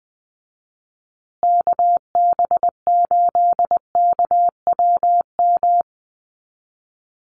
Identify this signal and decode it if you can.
Morse code, heard as a keyed tone, about 20 wpm, 700 Hz. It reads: KB8KWM
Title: USA Call Signs Repeated 3x 20wpm